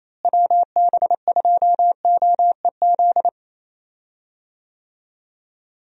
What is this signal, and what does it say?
Morse code, heard as a keyed tone, about 28 wpm, 700 Hz. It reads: WB2OEZ
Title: USA Call Signs - 6 Characters - Repeated 2x - 28wpm